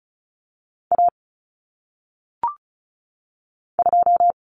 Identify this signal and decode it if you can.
Morse code, heard as a keyed tone, about 36 wpm, 700 Hz. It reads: A 2
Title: Common Cut Numbers - 1-Digit 35wpm